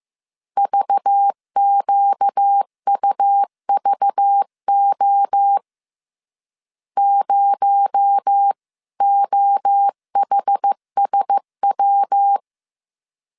Morse code, heard as a keyed tone, about 15 wpm, 800 Hz. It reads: VQUVO 0OHSW
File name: CW / DET / 1 6 1500C S B 14wpm 2958